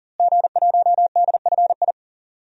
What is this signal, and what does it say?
Morse code, heard as a keyed tone, about 37 wpm, 700 Hz. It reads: G1DFI
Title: DXCC-100 Call Signs - 5 Characters - Repeated 3x - 40wpm